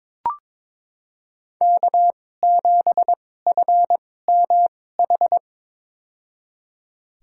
Morse code, heard as a keyed tone, about 22 wpm, spaced as for 16 wpm, 700 Hz. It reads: K7FMH
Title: USA Call Signs - 5 Characters - 2x Character Spacing - 22wpm